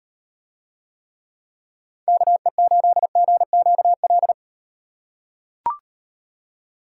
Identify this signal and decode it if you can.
Morse code, heard as a keyed tone, about 36 wpm, 700 Hz. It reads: KE8GQL